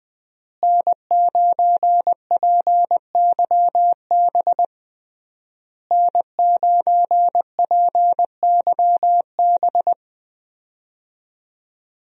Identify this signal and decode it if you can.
Morse code, heard as a keyed tone, about 20 wpm, 700 Hz. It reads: N9PYB N9PYB
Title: USA Call Signs - 5 Characters - Repeated 3x - 20wpm